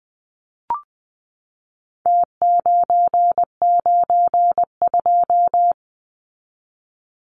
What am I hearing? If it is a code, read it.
Morse code, heard as a keyed tone, about 20 wpm, 700 Hz. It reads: T992
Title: Numbers - Contest Style - 4-Digit 20wpm